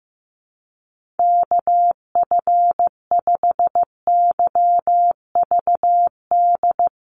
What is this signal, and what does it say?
Morse code, heard as a keyed tone, about 15 wpm, 700 Hz. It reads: KF5YVD
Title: USA Call Signs 15wpm